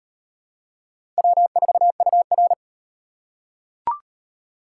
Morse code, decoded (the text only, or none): W4UR